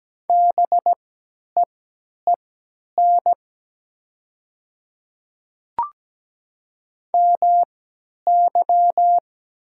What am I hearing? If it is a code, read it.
Morse code, heard as a keyed tone, about 17 wpm, spaced as for 10 wpm, 700 Hz. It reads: BEEN MY
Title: 3x Character Spacing - Top 100 Words - 17wpm